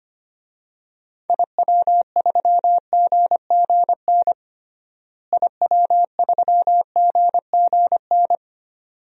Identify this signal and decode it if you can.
Morse code, heard as a keyed tone, about 25 wpm, 700 Hz. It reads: IW3GGN IW3GGN
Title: DXCC-100 Call Signs - 6 Characters - Repeated 2x - 25wpm